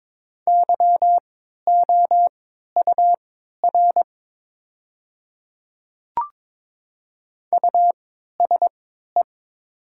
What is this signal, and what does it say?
Morse code, heard as a keyed tone, about 22 wpm, spaced as for 12 wpm, 690 Hz. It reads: YOUR USE
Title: 3x Character Spacing - Top 500 Words - 22wpm